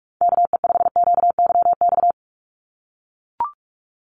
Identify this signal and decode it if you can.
Morse code, heard as a keyed tone, about 44 wpm, 700 Hz. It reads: KE5QYX
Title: USA Call Signs 45wpm